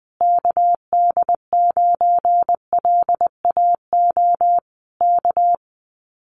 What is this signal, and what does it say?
Morse code, heard as a keyed tone, about 19 wpm, 700 Hz. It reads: KD9LAO K
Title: Common QSO Phrases Encore 20wpm